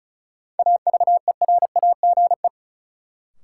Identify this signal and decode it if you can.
Morse code, heard as a keyed tone, about 34 wpm, 700 Hz. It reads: AVERAGE
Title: Sets of 5 Words - New Word Drill - 35wpm